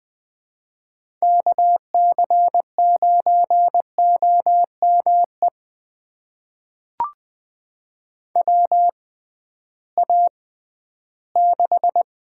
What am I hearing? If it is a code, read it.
Morse code, heard as a keyed tone, about 20 wpm, spaced as for 13 wpm, 700 Hz. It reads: KC9OME WA6
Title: USA Call Signs - 6 Characters - 6x Character Spacing - 20wpm